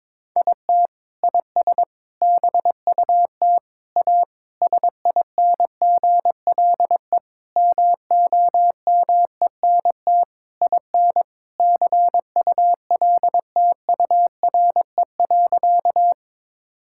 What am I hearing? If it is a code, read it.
Morse code, heard as a keyed tone, about 21 wpm, 700 Hz. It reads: IT IS BUT A SINGLE MOMENT IN CULTURE.
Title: Sentences from Top 1000 Words - Review Part 6 - 22wpm